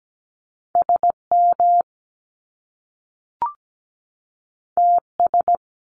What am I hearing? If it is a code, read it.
Morse code, heard as a keyed tone, about 17 wpm, 700 Hz. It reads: SM TS
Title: Top Two Letter Combinations in Words 17wpm